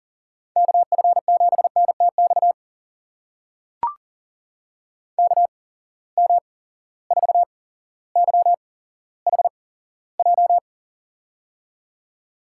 Morse code, heard as a keyed tone, about 40 wpm, spaced as for 17 wpm, 700 Hz. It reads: KF7NTX KM4YHJ
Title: USA Call Signs - 8x Character Spacing - 40wpm